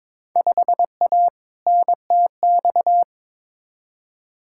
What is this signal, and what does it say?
Morse code, heard as a keyed tone, about 22 wpm, 700 Hz. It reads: 5A NTX
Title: Field Day Exchanges - 22wpm